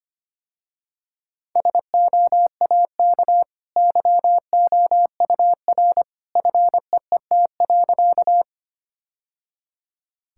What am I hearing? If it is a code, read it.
Morse code, heard as a keyed tone, about 25 wpm, 700 Hz. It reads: SOAK YOUR FEET.
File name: Easy Sentences with Hints - 25wpm